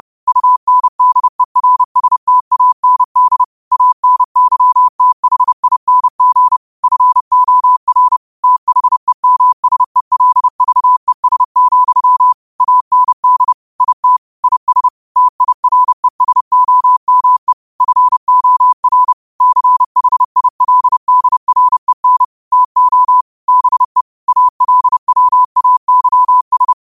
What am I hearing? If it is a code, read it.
Morse code, heard as a keyed tone, about 30 wpm, 1000 Hz. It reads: UNDERSTAND ANYTHING FOR THEMSELVES, AND IT IS TIRESOME FOR CHILDREN TO BE ALWAYS
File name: qotdfiles / 30wpm